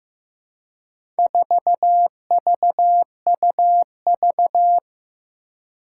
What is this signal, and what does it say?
Morse code, heard as a keyed tone, about 15 wpm, 700 Hz. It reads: 4VUV
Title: Four Challenging Letter-Number Combinations 15wpm